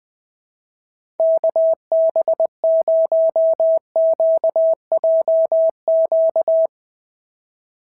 Morse code, heard as a keyed tone, about 20 wpm, 650 Hz. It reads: KB0QJQ
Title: SOTA QSO Practice 20wpm